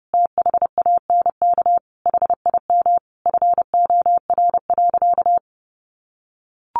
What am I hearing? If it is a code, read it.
Morse code, heard as a keyed tone, about 30 wpm, 700 Hz. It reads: THANK HIM FOR.
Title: Sentences from Top 1000 Words 30wpm